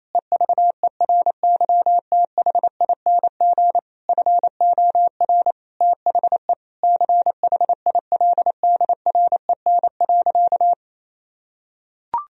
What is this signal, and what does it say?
Morse code, heard as a keyed tone, about 28 wpm, 700 Hz. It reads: EVERYTHING FOR THE CHILDREN.
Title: Sentences from Top 1000 Words - Review Part 2 - 28wpm